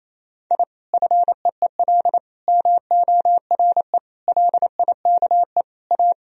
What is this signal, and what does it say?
Morse code, heard as a keyed tone, about 27 wpm, 700 Hz. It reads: I FEEL MORE LIKE A
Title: Sentences from Top 1000 Words - Review Part 2 - 28wpm